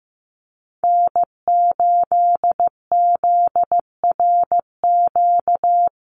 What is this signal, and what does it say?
Morse code, heard as a keyed tone, about 15 wpm, 700 Hz. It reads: N8ZRQ
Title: USA Call Signs Repeated 3x 15wpm